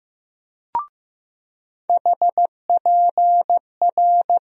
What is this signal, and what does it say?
Morse code, heard as a keyed tone, about 15 wpm, 700 Hz. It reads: HPR
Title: USA Call Sign Postfixes - 3 Letters 15wpm